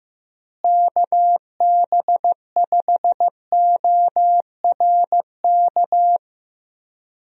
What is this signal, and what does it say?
Morse code, heard as a keyed tone, about 15 wpm, 700 Hz. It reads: KB5ORK
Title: USA Call Signs - 6 Characters - Repeated 2x - 15wpm